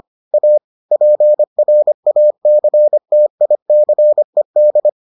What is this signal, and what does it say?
Morse code, heard as a keyed tone, about 25 wpm, 600 Hz. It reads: A PRACTICED